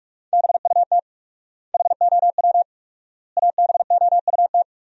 Morse code, heard as a keyed tone, about 46 wpm, 700 Hz. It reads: BUT HOW ABOUT
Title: 4x Word-Spacing - Sentences from Top 100 Words - 45wpm